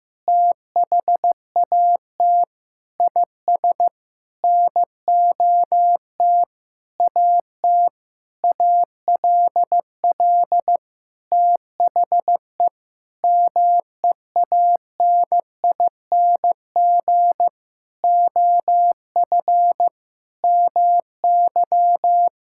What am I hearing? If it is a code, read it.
Morse code, heard as a keyed tone, about 15 wpm, 700 Hz. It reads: THAT IS NOT AT ALL THE MEANING OF MY
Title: Sentences from Top 1000 Words - Review Part 6 - 15wpm